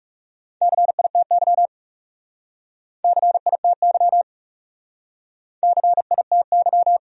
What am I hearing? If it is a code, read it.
Morse code, heard as a keyed tone, about 40 wpm, 700 Hz. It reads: CITY CITY CITY